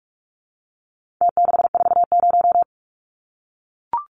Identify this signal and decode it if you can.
Morse code, heard as a keyed tone, about 44 wpm, 700 Hz. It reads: T640
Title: Numbers - Contest Style - 4-Digit 45wpm